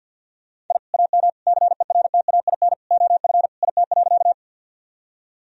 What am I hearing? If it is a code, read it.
Morse code, heard as a keyed tone, about 51 wpm, 700 Hz. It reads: I AM CERTAIN OF IT.
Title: Sentences from Top 1000 Words - Review Part 2 - 50wpm